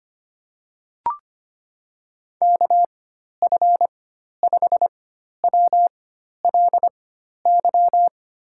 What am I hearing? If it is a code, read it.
Morse code, heard as a keyed tone, about 25 wpm, spaced as for 12 wpm, 700 Hz. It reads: KF5WLY
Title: USA Call Signs - 6 Characters - 4x Character Spacing - 25wpm